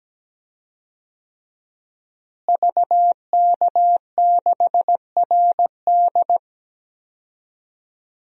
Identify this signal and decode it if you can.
Morse code, heard as a keyed tone, about 17 wpm, 700 Hz. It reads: VK6RD
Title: DXCC-100 Call Signs - 5 Characters - Repeated 3x - 17wpm